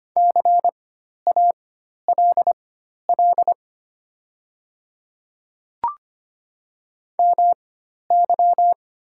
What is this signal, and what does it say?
Morse code, heard as a keyed tone, about 25 wpm, spaced as for 12 wpm, 700 Hz. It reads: CALL MY
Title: 4x Character Spacing - Top 100 Words - 25wpm